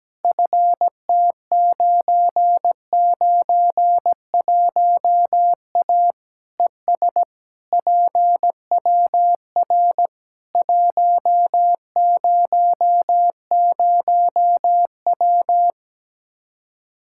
Morse code, heard as a keyed tone, about 17 wpm, 700 Hz. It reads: FT991A ES PWR 100W
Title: Practice QSO - Exchange 2 - Info - 17wpm